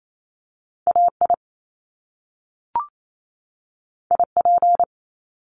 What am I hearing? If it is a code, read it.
Morse code, heard as a keyed tone, about 28 wpm, 700 Hz. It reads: AI IP
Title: Top Two Letter Combinations in Words 28wpm